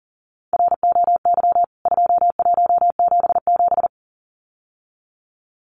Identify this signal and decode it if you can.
Morse code, heard as a keyed tone, about 40 wpm, 700 Hz. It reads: ROY 2177
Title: CWT Exchange 40wpm